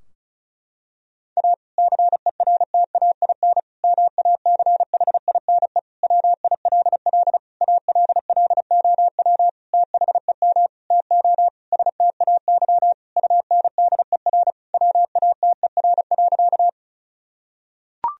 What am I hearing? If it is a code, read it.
Morse code, heard as a keyed tone, about 36 wpm, 700 Hz. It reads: A CERTAIN MACHINE WILL ALLOW THEM TO STAY UNDER WATER.
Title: Sentences from Top 1000 Words - Review Part 2 - 35wpm